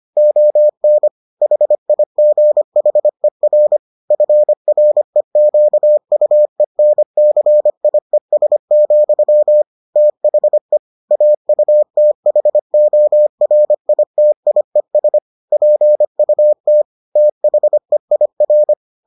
Morse code, heard as a keyed tone, about 25 wpm, 600 Hz. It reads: ON HIGHER FREQUENCIES, THE AUTHORITIES PUT THEIR